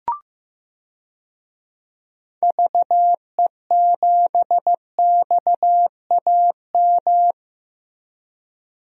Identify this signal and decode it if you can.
Morse code, heard as a keyed tone, about 15 wpm, 700 Hz. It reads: VE7XAM
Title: DXCC-100 Call Signs - 6 Characters - Repeated 3x - 15wpm